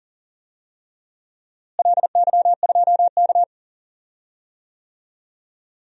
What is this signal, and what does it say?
Morse code, heard as a keyed tone, about 40 wpm, 700 Hz. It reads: LY2K
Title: DXCC-100 Call Signs - 4 Characters - Repeated 3x - 40wpm